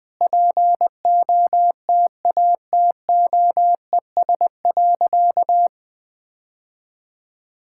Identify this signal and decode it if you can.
Morse code, heard as a keyed tone, about 20 wpm, 700 Hz. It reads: POTATOES.
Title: Easy Sentences with Hints - 20wpm